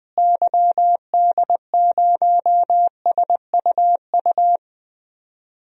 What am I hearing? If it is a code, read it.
Morse code, heard as a keyed tone, about 20 wpm, 700 Hz. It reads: YD0SUU
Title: DXCC-100 Call Signs - 6 Characters - Repeated 2x - 20wpm